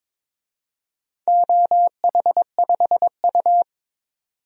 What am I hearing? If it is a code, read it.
Morse code, heard as a keyed tone, about 22 wpm, 700 Hz. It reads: OH5U
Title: DXCC-100 Call Signs - 4 Characters - 22wpm